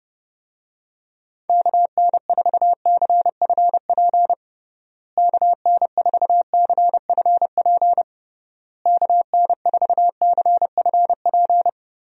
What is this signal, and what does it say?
Morse code, heard as a keyed tone, about 31 wpm, 700 Hz. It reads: KN4CFP KN4CFP KN4CFP
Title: USA Call Signs - 6 Characters - Repeated 3x - 30wpm